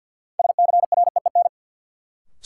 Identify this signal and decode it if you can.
Morse code, heard as a keyed tone, about 53 wpm, 700 Hz. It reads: SCREEN
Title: Sentences from Top 1000 Words - Review Part 6 - New Word Drill - 50wpm